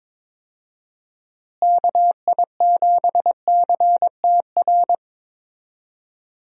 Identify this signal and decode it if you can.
Morse code, heard as a keyed tone, about 22 wpm, 700 Hz. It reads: KI7CTR
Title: USA Call Signs - 6 Characters - Repeated 2x - 22wpm